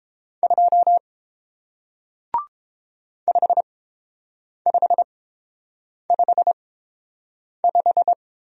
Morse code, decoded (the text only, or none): 2 5555